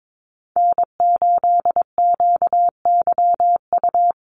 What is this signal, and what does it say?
Morse code, heard as a keyed tone, about 22 wpm, 700 Hz. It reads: N8QYU
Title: USA Call Signs Repeated 2x 22wpm